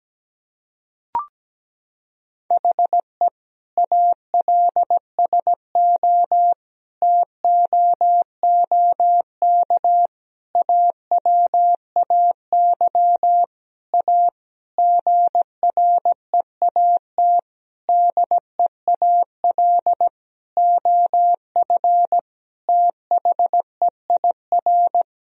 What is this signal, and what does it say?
Morse code, heard as a keyed tone, about 17 wpm, 700 Hz. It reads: HE ALSO TOOK AWAY A GREAT DEAL OF THEIR
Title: Sentences from Top 1000 Words - Review Part 1 - 17wpm